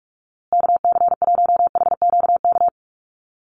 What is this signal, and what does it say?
Morse code, heard as a keyed tone, about 44 wpm, 700 Hz. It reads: KC1HQK
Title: USA Call Signs Repeated 3x 45wpm